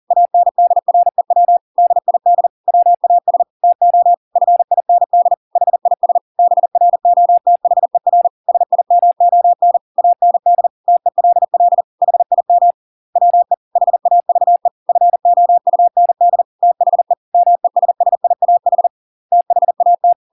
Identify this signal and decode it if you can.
Morse code, heard as a keyed tone, about 40 wpm, 700 Hz. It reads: ANDREW DID WAS TO FIND HIS BROTHER SIMON AND TELL HIM WE HAVE FOUND THE MESSIAH THAT